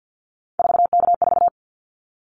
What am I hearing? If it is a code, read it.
Morse code, heard as a keyed tone, about 50 wpm, 700 Hz. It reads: 4K4
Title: Callsign Prefixes for Top 100 DXCC - 50wpm